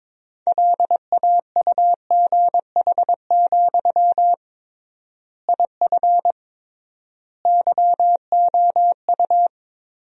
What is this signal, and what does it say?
Morse code, heard as a keyed tone, about 22 wpm, 700 Hz. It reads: LAUGH, IF YOU
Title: 3x Word-Spacing - Sentences from Top 500 Words - 22wpm